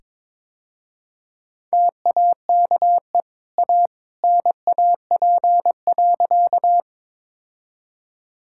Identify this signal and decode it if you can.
Morse code, heard as a keyed tone, about 22 wpm, 690 Hz. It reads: TAKE A NAP.
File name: practice / 22wpm / Easy Sentences with Hints - 22wpm